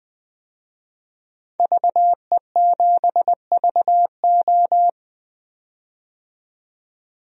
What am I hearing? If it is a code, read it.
Morse code, heard as a keyed tone, about 20 wpm, 700 Hz. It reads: VE7VO
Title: DXCC-100 Call Signs - 5 Characters - 20wpm